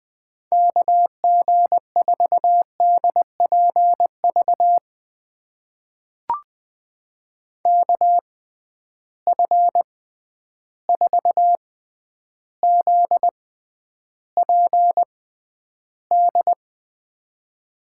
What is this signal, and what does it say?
Morse code, heard as a keyed tone, about 21 wpm, spaced as for 10 wpm, 700 Hz. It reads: KG4DPV KF4ZPD
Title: USA Call Signs - 6 Characters - 6x Character Spacing - 20wpm